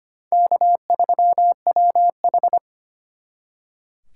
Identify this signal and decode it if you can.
Morse code, heard as a keyed tone, about 31 wpm, 700 Hz. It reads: K3WH